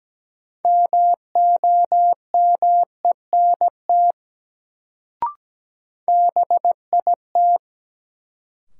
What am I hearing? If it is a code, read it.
Morse code, heard as a keyed tone, about 17 wpm, 700 Hz. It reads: MOMENT BIT
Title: Sets of 3 Words - New Word Drill - 17wpm